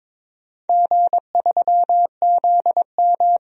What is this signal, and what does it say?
Morse code, heard as a keyed tone, about 22 wpm, 700 Hz. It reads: G3ZM
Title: DXCC-100 Call Signs - 4 Characters - Repeated 3x - 22wpm